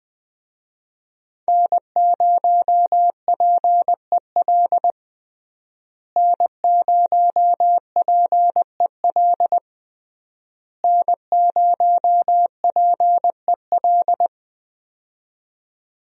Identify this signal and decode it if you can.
Morse code, heard as a keyed tone, about 20 wpm, 700 Hz. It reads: N0PEL N0PEL N0PEL
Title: USA Call Signs - 5 Characters - Repeated 3x - 20wpm